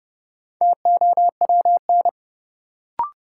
Morse code, heard as a keyed tone, about 30 wpm, 700 Hz. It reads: TOWN
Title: Sentences from Top 500 Words - New Word Drill - 30wpm